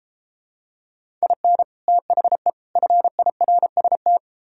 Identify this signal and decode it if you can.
Morse code, heard as a keyed tone, about 33 wpm, 700 Hz. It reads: IN THE FIRST